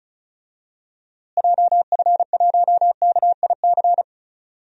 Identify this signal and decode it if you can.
Morse code, heard as a keyed tone, about 34 wpm, 700 Hz. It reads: JF1KIC
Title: DXCC-100 Call Signs - 6 Characters - Repeated 3x - 35wpm